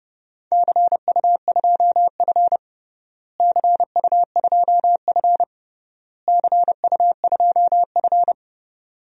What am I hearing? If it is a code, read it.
Morse code, heard as a keyed tone, about 30 wpm, 700 Hz. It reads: CU2F CU2F CU2F
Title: DXCC-100 Call Signs - 4 Characters - Repeated 3x - 30wpm